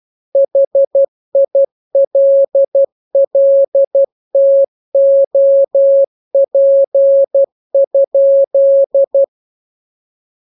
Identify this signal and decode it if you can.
Morse code, heard as a keyed tone, about 12 wpm, 550 Hz. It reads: HILLTOP?
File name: jokes / joke 5 12 0000